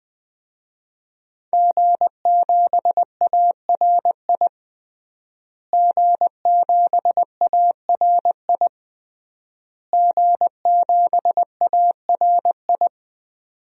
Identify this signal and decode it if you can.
Morse code, heard as a keyed tone, about 20 wpm, 700 Hz. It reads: G7ARI G7ARI G7ARI
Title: DXCC-100 Call Signs - 5 Characters - Repeated 3x - 20wpm